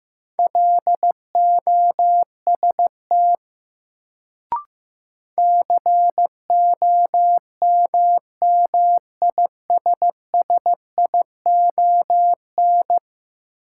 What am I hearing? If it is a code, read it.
Morse code, heard as a keyed tone, about 15 wpm, 700 Hz. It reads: LOST COMMISSION
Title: Sentences from Top 1400 Words - New Word Drill - 15wpm